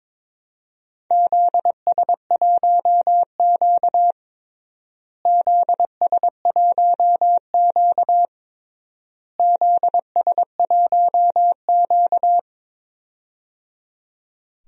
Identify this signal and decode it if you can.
Morse code, heard as a keyed tone, about 22 wpm, 700 Hz. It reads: ZS1Q ZS1Q ZS1Q
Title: DXCC-100 Call Signs - 4 Characters - Repeated 3x - 22wpm